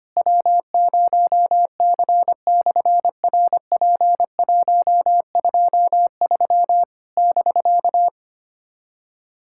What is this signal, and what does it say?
Morse code, heard as a keyed tone, about 25 wpm, 700 Hz. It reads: W0C/RP123 <BK>
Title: SOTA QSO Practice 25wpm